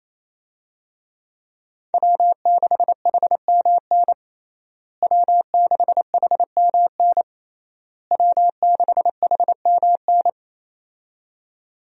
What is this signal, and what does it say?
Morse code, heard as a keyed tone, about 28 wpm, 700 Hz. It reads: W6HMN W6HMN W6HMN
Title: USA Call Signs - 5 Characters - Repeated 3x - 28wpm